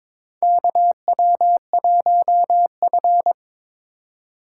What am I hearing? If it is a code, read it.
Morse code, heard as a keyed tone, about 21 wpm, 700 Hz. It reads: KW1F
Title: USA Call Signs - 4 Characters - 2x Character Spacing - 22wpm